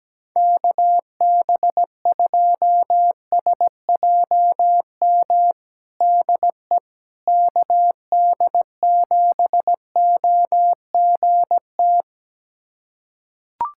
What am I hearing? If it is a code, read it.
Morse code, heard as a keyed tone, about 17 wpm, 700 Hz. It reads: KB2SJM DE KD7OGT
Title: USA Callsign DE Callsign - 6 Characters - 17wpm